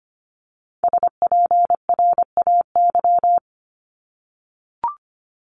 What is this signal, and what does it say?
Morse code, heard as a keyed tone, about 25 wpm, 700 Hz. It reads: SPRAY
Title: Multi-letter Phonemes Repeated 3x 25wpm